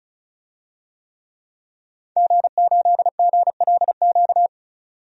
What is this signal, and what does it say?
Morse code, heard as a keyed tone, about 36 wpm, 700 Hz. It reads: G8GLQ
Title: DXCC-100 Call Signs - 5 Characters - 35wpm